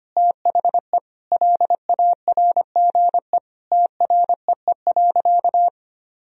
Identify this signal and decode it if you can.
Morse code, heard as a keyed tone, about 25 wpm, 700 Hz. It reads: THE LARGE TREE.
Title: Sentences from Top 1000 Words - Review Part 4 - 25wpm